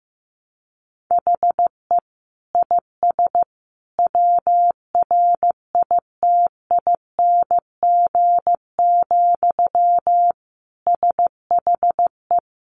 Morse code, heard as a keyed tone, about 15 wpm, 700 Hz. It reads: HE IS WRITING, SHE
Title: Sentences from Top 800 Words 15wpm